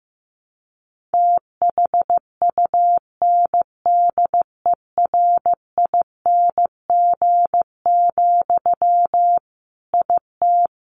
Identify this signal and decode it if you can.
Morse code, heard as a keyed tone, about 15 wpm, 700 Hz. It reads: THUNDERING, IT
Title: A Sound Of Thunder 15wpm